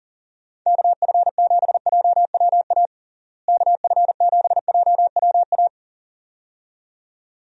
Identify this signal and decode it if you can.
Morse code, heard as a keyed tone, about 40 wpm, 700 Hz. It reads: KF7JWA KF7JWA
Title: USA Call Signs - 6 Characters - Repeated 2x - 40wpm